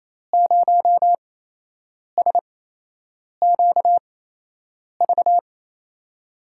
Morse code, heard as a keyed tone, about 27 wpm, spaced as for 8 wpm, 700 Hz. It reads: 0SQV
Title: USA Call Signs - 5 Characters - 8x Character Spacing - 28wpm